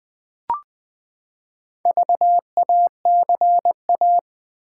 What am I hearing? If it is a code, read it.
Morse code, heard as a keyed tone, about 20 wpm, 700 Hz. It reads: VACA
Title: Build A Word - 4 Letters At A Time - 8-Letter Words - 20wpm